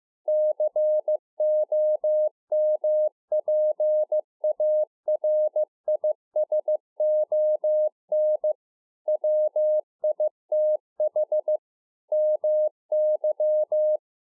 Morse code, heard as a keyed tone, about 15 wpm, 610 Hz. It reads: COMPARISON WITH MY
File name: 15WPM